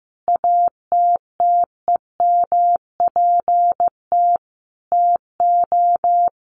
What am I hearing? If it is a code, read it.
Morse code, heard as a keyed tone, about 15 wpm, 700 Hz. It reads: ATTEMPT TO
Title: Sentences from Top 1400 Words 15wpm